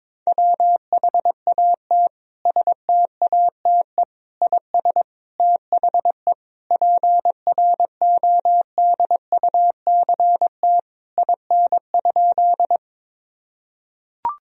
Morse code, heard as a keyed tone, about 22 wpm, 700 Hz. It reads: WHAT STATE IS THE PRODUCT IN?
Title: Sentences from Top 1000 Words - Review Part 2 - 22wpm